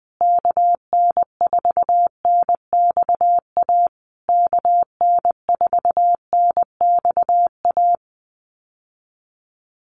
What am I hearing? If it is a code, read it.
Morse code, heard as a keyed tone, about 19 wpm, 700 Hz. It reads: KN4NXA KN4NXA
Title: USA Call Signs Repeated 3x 20wpm